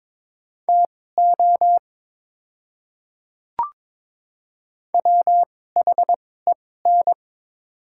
Morse code, heard as a keyed tone, about 22 wpm, spaced as for 16 wpm, 700 Hz. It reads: TO WHEN
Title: 2x Character Spacing - Top 100 Words - 22wpm